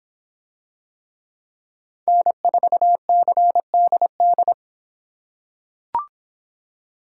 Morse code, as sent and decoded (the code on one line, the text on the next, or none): -. ....- -.-. -.. -..
N4CDD